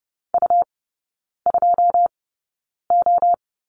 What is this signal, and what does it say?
Morse code, heard as a keyed tone, about 30 wpm, spaced as for 9 wpm, 700 Hz. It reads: U2O
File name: Lesson 18e - Letter m - Three Characters 30wpm